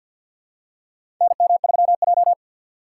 Morse code, heard as a keyed tone, about 52 wpm, 700 Hz. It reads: NM3J
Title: USA Call Signs - 2x Character Spacing - 50wpm